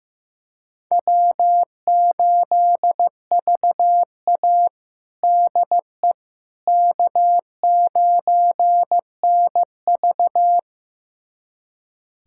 Morse code, heard as a keyed tone, about 15 wpm, 700 Hz. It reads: W8VA DE K9NV
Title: USA Callsign DE Callsign - 4 Characters - 15wpm